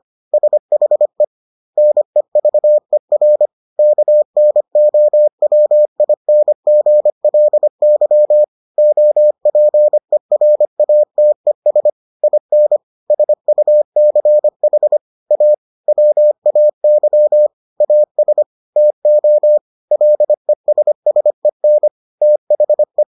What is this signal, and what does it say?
Morse code, heard as a keyed tone, about 25 wpm, 600 Hz. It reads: SHE NEVER KNOWINGLY OPERATES IN SUCH A WAY AS TO LESSEN THE